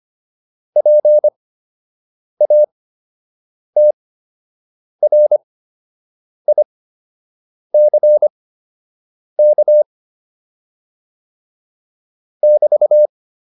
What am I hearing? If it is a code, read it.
Morse code, heard as a keyed tone, about 25 wpm, spaced as for 7 wpm, 600 Hz. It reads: PATRICK <BT>